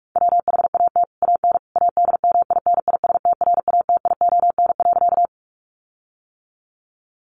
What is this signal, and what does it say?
Morse code, heard as a keyed tone, about 44 wpm, 700 Hz. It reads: WHAT AN ADMINISTRATION.
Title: Sentences from Top 1300 Words 45wpm